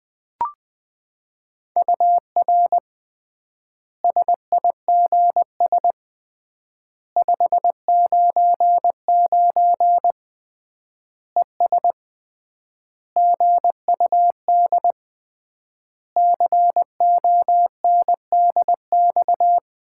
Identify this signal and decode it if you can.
Morse code, heard as a keyed tone, about 20 wpm, 700 Hz. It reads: UR SIGS 599 ES GUD CONDX
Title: Common QSO Phrases - 3x Word-Spacing - 20wpm